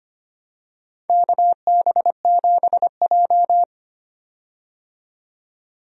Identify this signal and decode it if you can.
Morse code, heard as a keyed tone, about 25 wpm, 700 Hz. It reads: KB7J
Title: USA Call Signs - 4 Characters - 25wpm